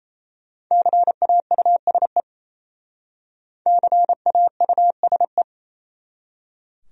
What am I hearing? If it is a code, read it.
Morse code, heard as a keyed tone, about 34 wpm, 700 Hz. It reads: CAUSE CAUSE
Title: Speed-Racing Top 200 Words 28wpm